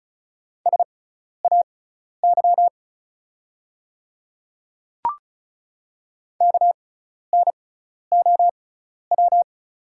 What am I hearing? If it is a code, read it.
Morse code, heard as a keyed tone, about 36 wpm, spaced as for 12 wpm, 700 Hz. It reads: SAY KNOW
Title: 6x Character Spacing - Top 200 Words - 35wpm